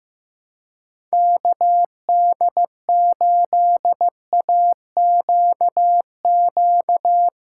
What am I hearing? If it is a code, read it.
Morse code, heard as a keyed tone, about 15 wpm, 700 Hz. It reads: KD8AQQ
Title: USA Call Signs - 6 Characters - 2x Character Spacing - 15wpm